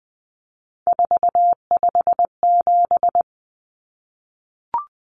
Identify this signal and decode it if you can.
Morse code, heard as a keyed tone, about 20 wpm, 700 Hz. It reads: 457
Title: Common Cut Numbers - 3-Digit 20wpm